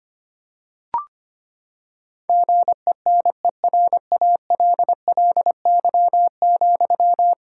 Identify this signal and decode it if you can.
Morse code, heard as a keyed tone, about 25 wpm, 700 Hz. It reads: GENERALLY,
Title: Sentences from Top 1000 Words - Review Part 6 - 25wpm